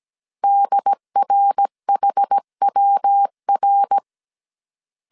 Morse code, heard as a keyed tone, about 17 wpm, 800 Hz. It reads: DRHWR
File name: CW / DET / 1 3 1500C S A 16wpm 2141